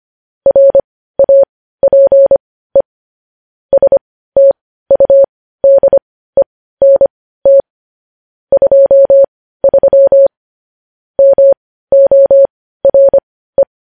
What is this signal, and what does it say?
Morse code, heard as a keyed tone, about 25 wpm, spaced as for 15 wpm, 550 Hz. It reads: RAPE STUDENT 23 MORE
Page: news headlines in morse code at 15 wpm